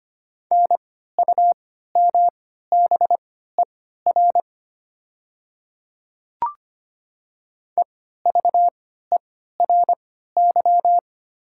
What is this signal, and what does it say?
Morse code, heard as a keyed tone, about 25 wpm, spaced as for 14 wpm, 700 Hz. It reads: NUMBER EVERY
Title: 3x Character Spacing - Top 200 Words - 25wpm